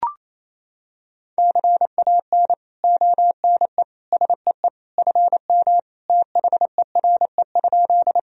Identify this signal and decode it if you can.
Morse code, heard as a keyed tone, about 27 wpm, 700 Hz. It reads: CAN ONE SEE FM THERE?
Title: Sentences from Top 100 Words and Abbreviations 28wpm